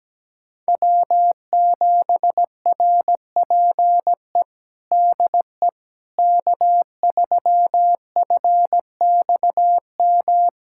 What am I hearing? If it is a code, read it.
Morse code, heard as a keyed tone, about 17 wpm, 700 Hz. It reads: W7RPE DE K3FXM
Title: USA Callsign DE Callsign - 5 Characters - 17wpm